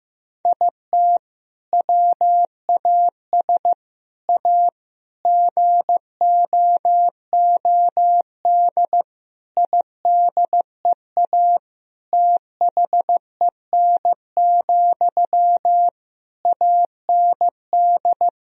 Morse code, heard as a keyed tone, about 15 wpm, 700 Hz. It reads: IT WAS A GOOD IDEA THEN, AND
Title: Sentences from Top 1000 Words - Review Part 4 - 15wpm